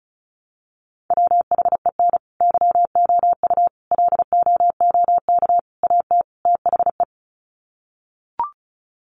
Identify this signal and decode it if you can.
Morse code, heard as a keyed tone, about 34 wpm, 700 Hz. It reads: WHEN YOU LOOK AT THE
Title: Sets of 5 Words 35wpm